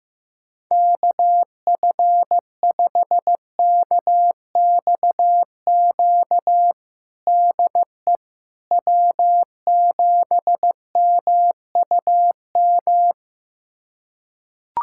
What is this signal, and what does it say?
Morse code, heard as a keyed tone, about 16 wpm, 700 Hz. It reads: KF5KXQ DE W7MUM
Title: USA Callsign DE Callsign 15wpm